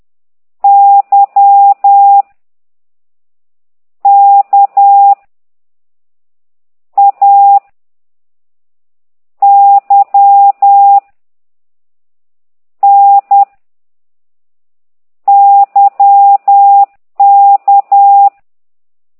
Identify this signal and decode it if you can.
Morse code, heard as a keyed tone, about 10 wpm, 800 Hz. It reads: Y K A Y N Y K
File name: ct2idw / exerc9